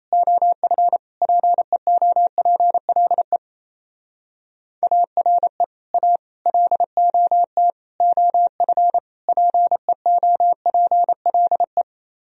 Morse code, heard as a keyed tone, about 34 wpm, 700 Hz. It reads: OF PEOPLE ARE A LOT OF PEOPLE
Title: Speed-Racing - Sets of 5 Words 28wpm